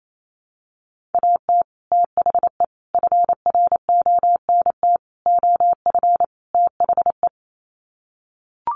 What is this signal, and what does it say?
Morse code, heard as a keyed tone, about 28 wpm, 700 Hz. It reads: AT THE FRONT OF THE
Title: Sets of 5 Words 28wpm